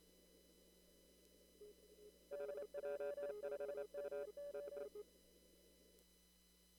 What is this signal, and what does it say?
Morse code, heard as a keyed tone, about 28 wpm, 580 Hz. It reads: DJ5MO
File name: CW / kt5j wpx cw-op-k5pi